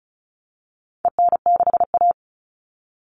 Morse code, heard as a keyed tone, about 34 wpm, 700 Hz. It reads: EN6A
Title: All Cut Numbers - 4-Digit 35wpm